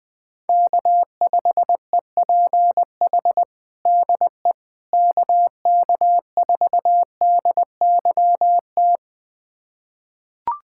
Morse code, heard as a keyed tone, about 20 wpm, 700 Hz. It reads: K5EPH DE KK4DYT
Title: USA Callsign DE Callsign 20wpm